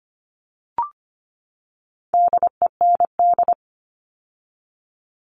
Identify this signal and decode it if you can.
Morse code, heard as a keyed tone, about 25 wpm, 700 Hz. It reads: DEND
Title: All Cut Numbers - 4-Digit 25wpm